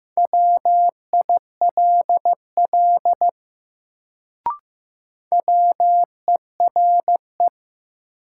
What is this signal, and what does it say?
Morse code, heard as a keyed tone, about 15 wpm, 700 Hz. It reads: WILL WERE
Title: Sentences from Top 100 Words - New Word Drill - 15wpm